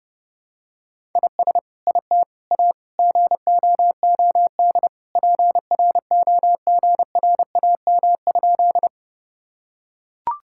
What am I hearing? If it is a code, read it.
Morse code, heard as a keyed tone, about 30 wpm, 700 Hz. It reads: IS IT A GOOD PROGRAM?
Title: Sentences from Top 1000 Words - Review Part 5 - 30wpm